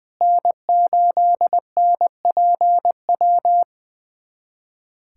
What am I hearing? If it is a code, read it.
Morse code, heard as a keyed tone, about 21 wpm, 700 Hz. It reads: N8NPW
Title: USA Callsign DE Callsign - 5 Characters - 20wpm